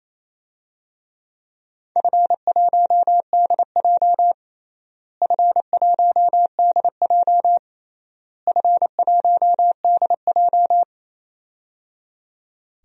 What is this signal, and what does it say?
Morse code, heard as a keyed tone, about 27 wpm, 700 Hz. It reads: F1DJ F1DJ F1DJ
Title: DXCC-100 Call Signs - 4 Characters - Repeated 3x - 28wpm